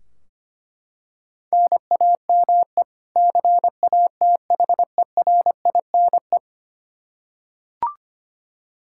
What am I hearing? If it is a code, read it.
Morse code, heard as a keyed tone, about 25 wpm, 700 Hz. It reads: NAME CATHERINE
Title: Practice QSO - Exchange 1 - Beginning - 25wpm